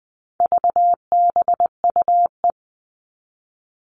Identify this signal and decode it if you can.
Morse code, heard as a keyed tone, about 20 wpm, 700 Hz. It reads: VBUE
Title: All Cut Numbers - 4-Digit 20wpm